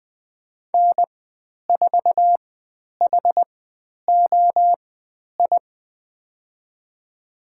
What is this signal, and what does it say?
Morse code, heard as a keyed tone, about 20 wpm, spaced as for 10 wpm, 700 Hz. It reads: N4HOI
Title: 109 Lesson 18g - Letter m - Call Signs 20wpm